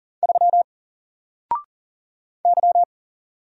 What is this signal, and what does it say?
Morse code, heard as a keyed tone, about 40 wpm, 700 Hz. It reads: 3 Y
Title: Single Character (Letter:Number:Punctuation:Prosign) 40wpm